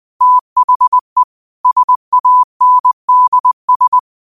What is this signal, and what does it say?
Morse code, heard as a keyed tone, about 20 wpm, 1000 Hz. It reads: THE SANDS